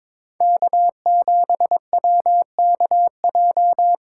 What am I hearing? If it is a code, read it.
Morse code, heard as a keyed tone, about 22 wpm, 700 Hz. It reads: K7WKJ
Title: USA Call Signs - 5 Characters - 4x Character Spacing - 22wpm